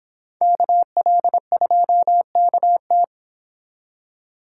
Morse code, heard as a keyed tone, about 29 wpm, 700 Hz. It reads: KL2KT